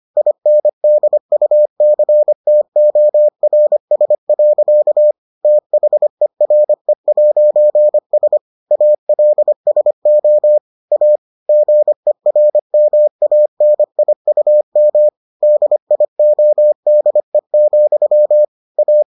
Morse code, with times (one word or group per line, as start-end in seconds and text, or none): 0.2-5.1 INDUCTORS.
5.4-8.4 THERE'S
8.7-10.6 ALSO
10.9-11.2 A
11.5-15.1 GERMANIUM
15.4-18.4 DIODE,
18.8-19.0 A